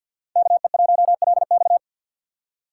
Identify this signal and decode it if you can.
Morse code, heard as a keyed tone, about 52 wpm, 700 Hz. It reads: KE1LX
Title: USA Call Signs - 3x Character Spacing - 50wpm